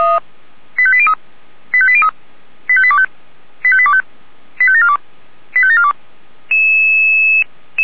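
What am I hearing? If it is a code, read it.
Five-tone selective call (ZVEI), heard as quick runs of five tones, tones of about 70 ms. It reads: 85803 85803 85825 85825 85742 85742
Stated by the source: ZVEI five-tone call 85827, received by SDR.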